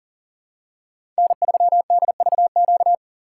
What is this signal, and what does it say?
Morse code, heard as a keyed tone, about 40 wpm, 700 Hz. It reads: N3DVQ
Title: USA Call Signs - 5 Characters - 4x Character Spacing - 40wpm